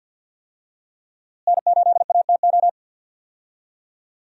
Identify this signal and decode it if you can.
Morse code, heard as a keyed tone, about 52 wpm, 700 Hz. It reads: N8ATO
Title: USA Call Signs - 5 Characters - Repeated 3x - 50wpm